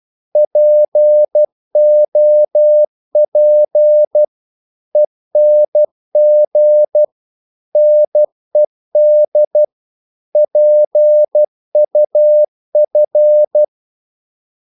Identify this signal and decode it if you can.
Morse code, heard as a keyed tone, about 12 wpm, 600 Hz. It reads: POP ENG NED PUF
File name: Korte ord DK 12wpm